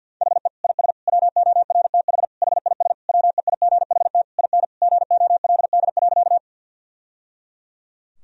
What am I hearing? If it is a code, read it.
Morse code, heard as a keyed tone, about 50 wpm, 700 Hz. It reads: HE IS WORTH HIS WEIGHT IN GOLD.
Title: Sentences from Top 1000 Words - Review Part 3 - 50wpm